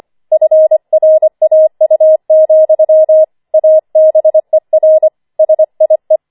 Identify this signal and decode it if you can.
Morse code, heard as a keyed tone, about 25 wpm, 630 Hz. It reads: FRAU, ABER SIE
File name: 25wpm